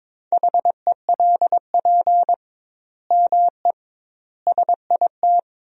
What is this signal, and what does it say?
Morse code, heard as a keyed tone, about 22 wpm, 700 Hz. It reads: HELP ME SIT
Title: 2x Word-Spacing - Sentences from Top 500 Words - 22wpm